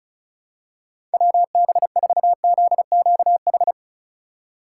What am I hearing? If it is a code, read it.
Morse code, heard as a keyed tone, about 34 wpm, 700 Hz. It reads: WB4ZQH
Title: USA Call Signs - 6 Characters - 8x Character Spacing - 35wpm